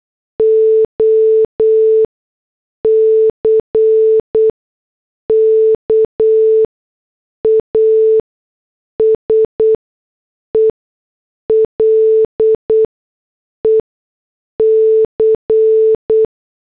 Morse code, heard as a keyed tone, about 8 wpm, spaced as for 6 wpm, 440 Hz. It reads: OCK A SELEC